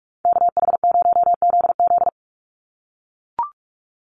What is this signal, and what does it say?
Morse code, heard as a keyed tone, about 44 wpm, 700 Hz. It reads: KH0ZZ
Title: Call Signs - 45wpm